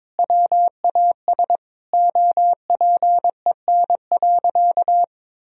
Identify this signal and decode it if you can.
Morse code, heard as a keyed tone, about 22 wpm, 700 Hz. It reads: WAS OPEN.
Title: Sentences from Top 1000 Words - Review Part 1 - 22wpm